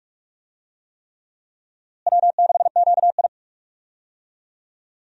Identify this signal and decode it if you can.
Morse code, heard as a keyed tone, about 46 wpm, 700 Hz. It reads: W6QI
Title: USA Call Signs - 4 Characters - Repeated 3x - 45wpm